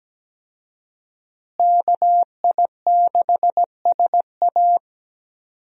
Morse code, heard as a keyed tone, about 17 wpm, 700 Hz. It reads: KI6SA
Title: USA Call Signs - 5 Characters - Repeated 2x - 17wpm